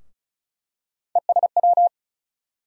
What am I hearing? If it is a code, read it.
Morse code, heard as a keyed tone, about 36 wpm, 700 Hz. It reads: ESW
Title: USA Call Sign Postfixes - 3 Letters 35wpm